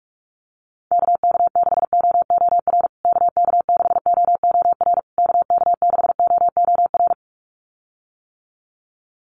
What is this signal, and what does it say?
Morse code, heard as a keyed tone, about 44 wpm, 700 Hz. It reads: KK6OOR KK6OOR KK6OOR
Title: USA Call Signs Repeated 3x 45wpm